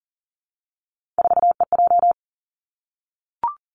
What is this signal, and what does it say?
Morse code, heard as a keyed tone, about 40 wpm, 700 Hz. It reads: 4EJ
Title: Three Challenging Letter-Number Combinations 40wpm